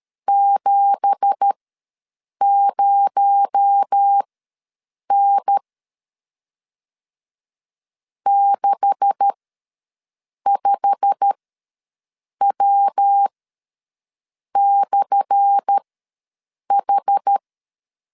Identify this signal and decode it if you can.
Morse code, heard as a keyed tone, about 13 wpm, spaced as for 6 wpm, 800 Hz. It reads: 70N 65W/H
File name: LEZIONI / 10 SIGNS / 10A SLASH ws 12wpm 3443